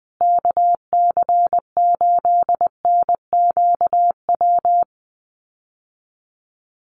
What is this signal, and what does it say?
Morse code, heard as a keyed tone, about 20 wpm, 700 Hz. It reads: KC8NQW
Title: USA Call Signs Repeated 3x 20wpm